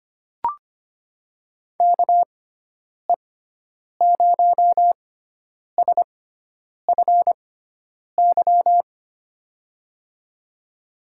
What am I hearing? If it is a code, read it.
Morse code, heard as a keyed tone, about 25 wpm, spaced as for 9 wpm, 700 Hz. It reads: KE0SFY
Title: USA Call Signs - 6 Characters - 6x Character Spacing - 25wpm